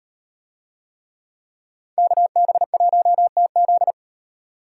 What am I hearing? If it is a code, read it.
Morse code, heard as a keyed tone, about 38 wpm, 700 Hz. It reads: KB1TZ